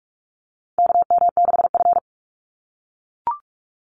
Morse code, heard as a keyed tone, about 44 wpm, 700 Hz. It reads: KM6F
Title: USA Call Signs 45wpm